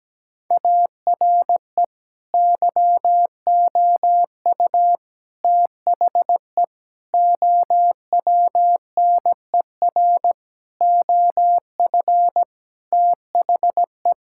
Morse code, heard as a keyed tone, about 17 wpm, 700 Hz. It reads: ARE YOU THE OWNER OF THE
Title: Sentences from Top 1600 Words 17wpm